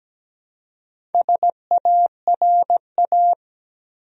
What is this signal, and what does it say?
Morse code, heard as a keyed tone, about 17 wpm, 700 Hz. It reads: SARA
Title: USA Common Names - 17wpm